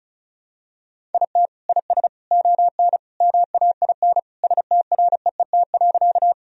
Morse code, heard as a keyed tone, about 36 wpm, 700 Hz. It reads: IT IS ON MAIN STREET.
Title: Sentences from Top 1000 Words - Review Part 2 - 35wpm